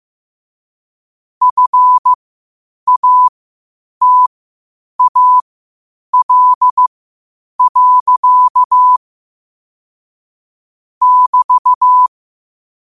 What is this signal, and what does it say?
Morse code, heard as a keyed tone, about 15 wpm, spaced as for 8 wpm, 1000 Hz. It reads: FATAL. <BT>
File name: qotdfiles / 8wpm